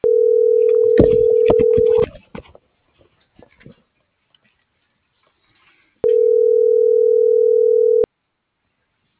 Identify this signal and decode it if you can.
Ringback tone (North America), 440 + 480 Hz.